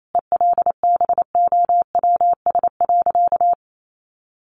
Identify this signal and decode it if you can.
Morse code, heard as a keyed tone, about 28 wpm, 700 Hz. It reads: ELBOWS.
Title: A Sound Of Thunder 28wpm